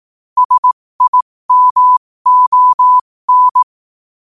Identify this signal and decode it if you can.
Morse code, heard as a keyed tone, about 18 wpm, spaced as for 15 wpm, 1000 Hz. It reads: SIMON
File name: qotdfiles / 15wpm